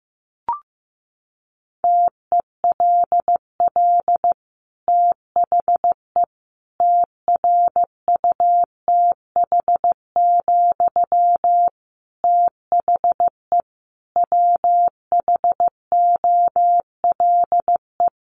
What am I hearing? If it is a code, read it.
Morse code, heard as a keyed tone, about 15 wpm, 700 Hz. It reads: TELL THE TRUTH, THE WHOLE
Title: Sentences from Top 800 Words 15wpm